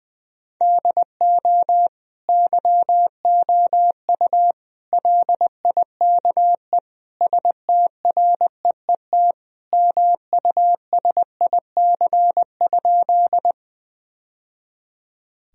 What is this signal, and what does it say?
Morse code, heard as a keyed tone, about 20 wpm, 700 Hz. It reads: DO YOU LIKE STREET MUSIC?
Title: Sentences from Top 1000 Words - Review Part 3 - 20wpm